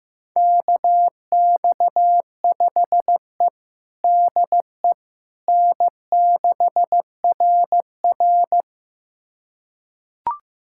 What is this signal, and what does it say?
Morse code, heard as a keyed tone, about 15 wpm, 700 Hz. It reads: KX5E DE N6RR
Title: USA Callsign DE Callsign - 4 Characters - 15wpm